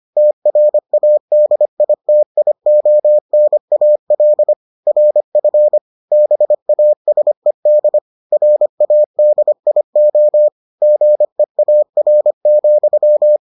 Morse code, heard as a keyed tone, about 25 wpm, 600 Hz. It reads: TRADITIONAL RF BASED RADIO GEAR,